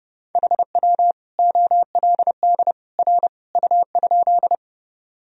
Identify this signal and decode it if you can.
Morse code, heard as a keyed tone, about 30 wpm, 700 Hz. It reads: HW OLD R U?
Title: Sentences from Top 200 Words and Abbreviations 30wpm